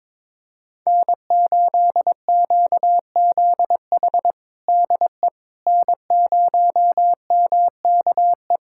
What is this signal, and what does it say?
Morse code, heard as a keyed tone, about 22 wpm, 700 Hz. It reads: N8QZH DE N0MKE
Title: USA Callsign DE Callsign - 5 Characters - 22wpm